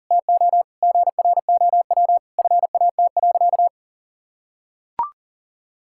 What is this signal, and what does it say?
Morse code, heard as a keyed tone, about 40 wpm, 700 Hz. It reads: TO GROW FAT.
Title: Sentences from Top 1000 Words - Review Part 5 - 40wpm